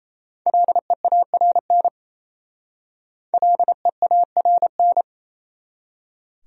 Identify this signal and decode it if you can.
Morse code, heard as a keyed tone, about 34 wpm, 700 Hz. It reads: LEARN LEARN
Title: Speed-Racing Top 500 Words 28wpm